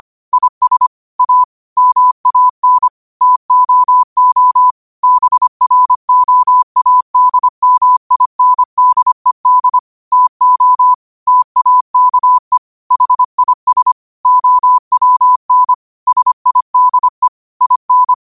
Morse code, heard as a keyed tone, about 25 wpm, 1000 Hz. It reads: IS A MAN TOO BROADMINDED TO TAKE HIS OWN SIDE IN